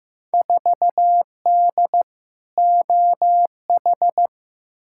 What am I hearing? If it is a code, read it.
Morse code, heard as a keyed tone, about 15 wpm, 700 Hz. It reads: 4D OH
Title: Field Day Exchanges Encore - 15wpm